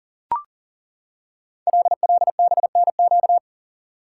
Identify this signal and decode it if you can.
Morse code, heard as a keyed tone, about 42 wpm, 690 Hz. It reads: LLBNQ
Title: Code Groups - 5 Letters - 40wpm